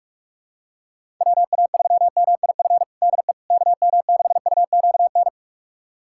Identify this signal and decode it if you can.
Morse code, heard as a keyed tone, about 42 wpm, 700 Hz. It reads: WA3MIF DE KM6UQN
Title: USA Callsign DE Callsign - 6 Characters - 45wpm